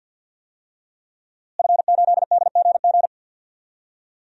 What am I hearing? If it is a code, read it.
Morse code, heard as a keyed tone, about 49 wpm, 700 Hz. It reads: F8DGG
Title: DXCC-100 Call Signs - 5 Characters - Repeated 3x - 50wpm